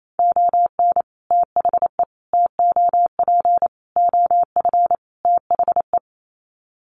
Morse code, heard as a keyed tone, about 28 wpm, 690 Hz. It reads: ON THE TOP OF THE
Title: Sets of 5 Words Encore 28wpm